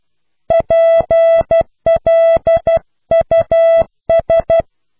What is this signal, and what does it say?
Morse code, heard as a keyed tone, about 12 wpm, 660 Hz. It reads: PLUS